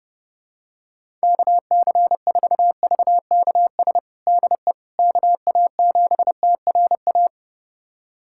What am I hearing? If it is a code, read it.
Morse code, heard as a keyed tone, about 32 wpm, 700 Hz. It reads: KC4VKS DE KA7TRA
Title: USA Callsign DE Callsign 30wpm